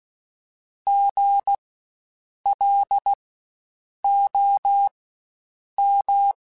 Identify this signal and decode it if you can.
Morse code, heard as a keyed tone, about 16 wpm, spaced as for 7 wpm, 800 Hz. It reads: GLOM
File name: Spaced Alphas / Spaced Alphas mp3s / Gs sp @15